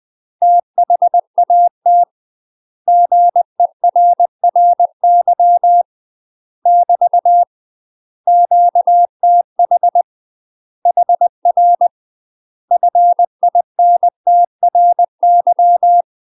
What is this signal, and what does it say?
Morse code, heard as a keyed tone, about 20 wpm, 700 Hz. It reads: THAT GERRY <BT> QTH HR FINTRY
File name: UK MASTER QSOs TRAINING / 20WPM / UK MASTER QSOs 0050